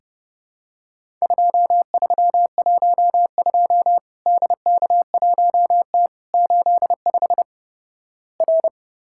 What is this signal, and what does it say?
Morse code, heard as a keyed tone, about 29 wpm, 700 Hz. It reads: 2312 DK1T 85 R
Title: Worked All Europe DX Contest - QTC Exchange - 30wpm